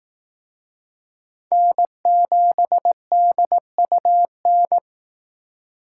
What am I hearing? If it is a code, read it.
Morse code, heard as a keyed tone, about 18 wpm, 700 Hz. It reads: N7DUN
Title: Speed Racing - USA 1x3 Call Signs 15wpm